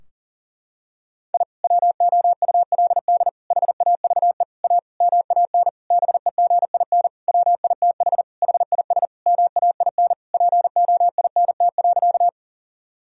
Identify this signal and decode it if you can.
Morse code, heard as a keyed tone, about 40 wpm, 700 Hz. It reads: I WOULD HAVE A MAN BEGIN WITH HIS MAIN POINT.
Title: Sentences from Top 1000 Words - Review Part 2 - 40wpm